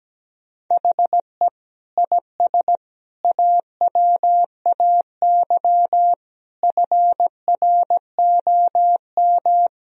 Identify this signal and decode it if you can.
Morse code, heard as a keyed tone, about 17 wpm, 700 Hz. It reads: HE IS AWAY FROM
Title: Sentences from Top 1000 Words - Review Part 1 - 17wpm